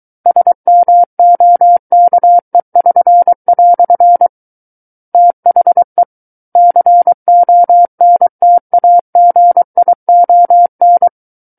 Morse code, heard as a keyed tone, about 23 wpm, 700 Hz. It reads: SMOKE<SN>" THE CONTAGION
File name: Book One / 14-Chapter 14 - 23 WPM